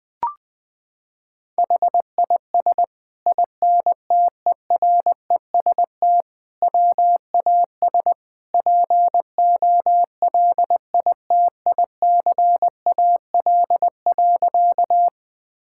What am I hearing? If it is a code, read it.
Morse code, heard as a keyed tone, about 21 wpm, 700 Hz. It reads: HIS INTEREST WAS POLITICAL.
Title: Sentences from Top 1000 Words - Review Part 5 - 20wpm